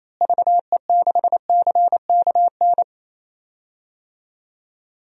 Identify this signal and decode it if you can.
Morse code, heard as a keyed tone, about 28 wpm, 700 Hz. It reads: VE6CKN
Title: DXCC-100 Call Signs - 6 Characters - Repeated 3x - 28wpm